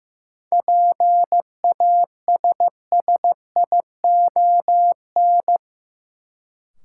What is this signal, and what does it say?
Morse code, heard as a keyed tone, about 16 wpm, 700 Hz. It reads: PASSION
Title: Sentences from Top 1300 Words - New Word Drill Alt - 15wpm